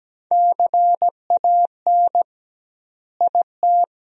Morse code, heard as a keyed tone, about 17 wpm, 700 Hz. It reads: CAN IT
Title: 2x Word-Spacing - Sentences from Top 100 Words - 17wpm